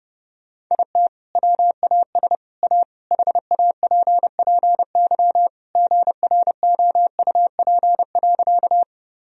Morse code, heard as a keyed tone, about 30 wpm, 700 Hz. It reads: IT WAS A HAPPY GROUP.
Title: Sentences from Top 1000 Words - Review Part 1 - 30wpm